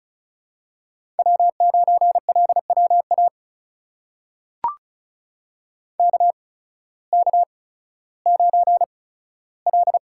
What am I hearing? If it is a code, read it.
Morse code, heard as a keyed tone, about 36 wpm, spaced as for 16 wpm, 700 Hz. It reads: W9LWA KK9L
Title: USA Call Signs - 8x Character Spacing - 35wpm